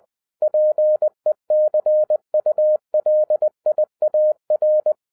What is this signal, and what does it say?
Morse code, heard as a keyed tone, about 20 wpm, 610 Hz. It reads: PECULIAR
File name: 2012-01-mp3 / n05-CAME BACK TO HAM RADIO AFTER 30 YEARS OF INTERVAL-20wpm